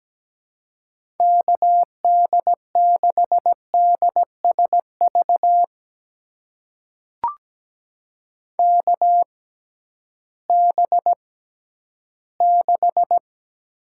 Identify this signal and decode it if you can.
Morse code, heard as a keyed tone, about 17 wpm, spaced as for 11 wpm, 700 Hz. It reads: KD6DSV KB6
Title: USA Call Signs - 6 Characters - 6x Character Spacing - 17wpm